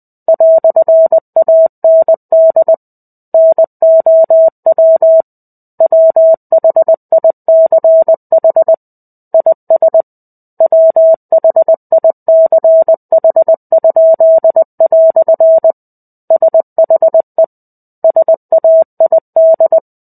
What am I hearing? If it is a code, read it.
Morse code, heard as a keyed tone, about 20 wpm, 650 Hz. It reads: "AND NOW WHICH IS WHICH?" SHE SAID
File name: 05-Chapter-05 20 WPM